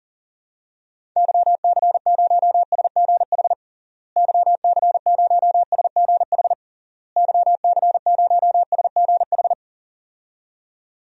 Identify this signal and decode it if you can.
Morse code, heard as a keyed tone, about 40 wpm, 700 Hz. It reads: YC0SGH YC0SGH YC0SGH
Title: DXCC-100 Call Signs - 6 Characters - Repeated 3x - 40wpm